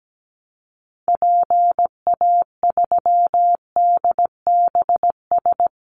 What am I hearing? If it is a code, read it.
Morse code, heard as a keyed tone, about 17 wpm, 700 Hz. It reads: PA3DBS
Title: Call Signs Repeated 3x - 17wpm